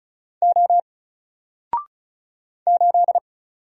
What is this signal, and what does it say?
Morse code, heard as a keyed tone, about 36 wpm, 700 Hz. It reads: O 8
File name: Single-Letter-Number-35wpm